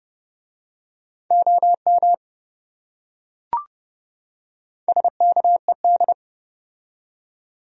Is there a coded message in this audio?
Yes: Morse code from 1.3 to 6.1 s, heard as a keyed tone, about 28 wpm, 700 Hz. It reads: OM SKED